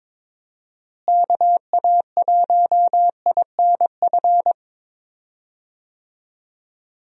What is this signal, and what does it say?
Morse code, heard as a keyed tone, about 22 wpm, 700 Hz. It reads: KA1INF
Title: USA Call Signs - 6 Characters - Repeated 3x - 22wpm